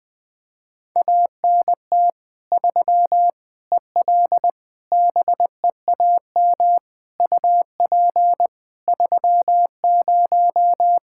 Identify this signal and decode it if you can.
Morse code, heard as a keyed tone, about 21 wpm, 700 Hz. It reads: ANT 3 EL BEAM UP 30
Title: Practice QSO - Exchange 2 - Info - 20wpm